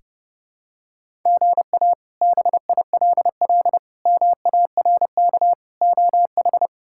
Morse code, heard as a keyed tone, about 30 wpm, 700 Hz. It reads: GA BILL MARK OH
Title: K1USN SST QSO Practice - 30wpm